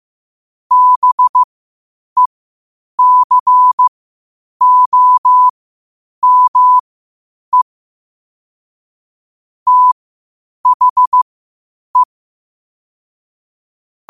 Morse code, heard as a keyed tone, about 15 wpm, spaced as for 8 wpm, 1000 Hz. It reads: BECOME THE
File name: qotdfiles / 8wpm